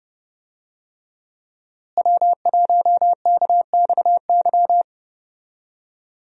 Morse code, heard as a keyed tone, about 29 wpm, 700 Hz. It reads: W1KXY
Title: USA Call Signs - 5 Characters - Repeated 3x - 30wpm